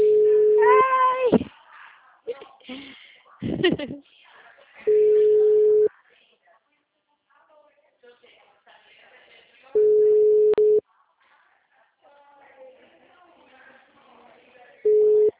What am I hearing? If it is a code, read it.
Ringback tone (Europe, 425 Hz), with speech.
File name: out- -2013- - - .58922